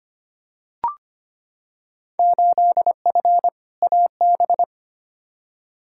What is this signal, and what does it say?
Morse code, heard as a keyed tone, about 25 wpm, 700 Hz. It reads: 8F AB
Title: Field Day Exchanges Encore - 25wpm